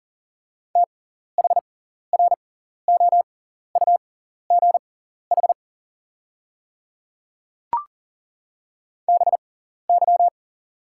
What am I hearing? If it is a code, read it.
Morse code, heard as a keyed tone, about 40 wpm, spaced as for 14 wpm, 700 Hz. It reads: THROUGH BY
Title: 6x Character Spacing - Top 200 Words - 40wpm